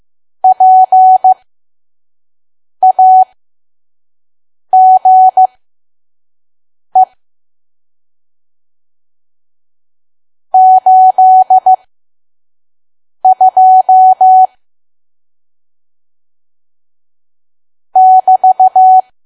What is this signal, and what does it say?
Morse code, heard as a keyed tone, about 15 wpm, spaced as for 5 wpm, 750 Hz. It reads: PAGE 82 <BT>
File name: Archive / 5 WPM files / PM